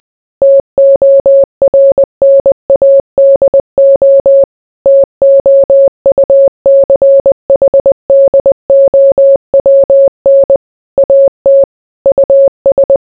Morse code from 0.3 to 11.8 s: TORNADO TOUCHDOWN AT